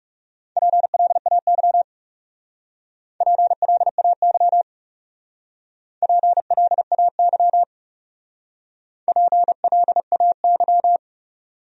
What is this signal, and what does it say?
Morse code, heard as a keyed tone, about 37 wpm, 700 Hz. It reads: PLAY PLAY PLAY PLAY
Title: Speed-Racing Top 500 Words 30wpm